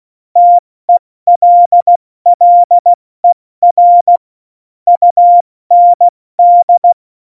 Morse code, heard as a keyed tone, about 16 wpm, spaced as for 14 wpm, 700 Hz. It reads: TELLER UND
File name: Morsekurs / Witze / Witz 17.Dezember 2019, 70 F80